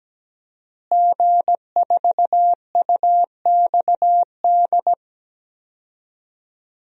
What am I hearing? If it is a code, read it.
Morse code, heard as a keyed tone, about 17 wpm, 700 Hz. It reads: G4UXD
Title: DXCC-100 Call Signs - 5 Characters - Repeated 2x - 17wpm